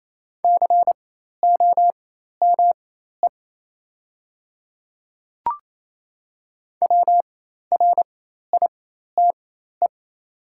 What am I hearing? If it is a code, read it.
Morse code, heard as a keyed tone, about 28 wpm, spaced as for 13 wpm, 700 Hz. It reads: COME WRITE
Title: 4x Character Spacing - Top 300 Words - 28wpm